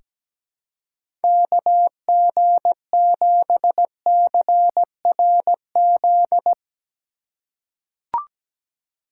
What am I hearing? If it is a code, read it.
Morse code, heard as a keyed tone, about 17 wpm, 700 Hz. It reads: KG7CRZ
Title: USA Call Signs - 3x Character Spacing - 17wpm